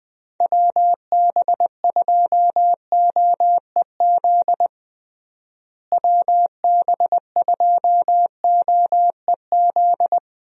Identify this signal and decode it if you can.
Morse code, heard as a keyed tone, about 20 wpm, 700 Hz. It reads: WB2OEZ WB2OEZ
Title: USA Call Signs - 6 Characters - Repeated 2x - 20wpm